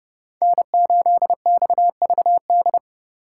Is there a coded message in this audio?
Yes: Morse code from 0.4 to 2.8 s, heard as a keyed tone, about 29 wpm, 700 Hz. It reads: N8XVD